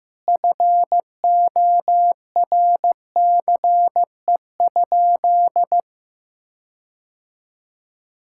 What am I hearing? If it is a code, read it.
Morse code, heard as a keyed tone, about 15 wpm, 700 Hz. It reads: FORCE?
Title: Sentences from Top 1600 Words 15wpm